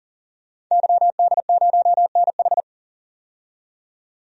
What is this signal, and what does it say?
Morse code, heard as a keyed tone, about 40 wpm, 700 Hz. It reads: YD0NH
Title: DXCC-100 Call Signs - 5 Characters - 40wpm